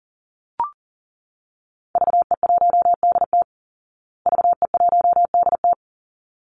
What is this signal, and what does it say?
Morse code, heard as a keyed tone, about 40 wpm, 700 Hz. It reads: VE1DT VE1DT
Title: Call Signs Repeated 3x - 40wpm